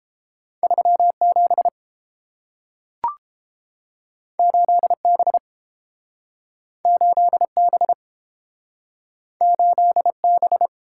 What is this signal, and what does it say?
Morse code, heard as a keyed tone, about 32 wpm, 700 Hz. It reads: 37 8B 8B 8B